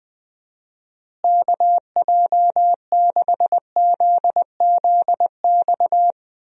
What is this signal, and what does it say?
Morse code, heard as a keyed tone, about 20 wpm, 700 Hz. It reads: KJ6ZZX
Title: USA Call Signs - 6 Characters - Repeated 2x - 20wpm